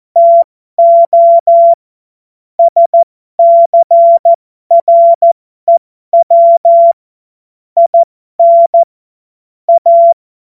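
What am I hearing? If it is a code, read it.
Morse code, heard as a keyed tone, about 14 wpm, spaced as for 12 wpm, 680 Hz. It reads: TO SCREW IN A
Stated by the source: Morse code joke practice.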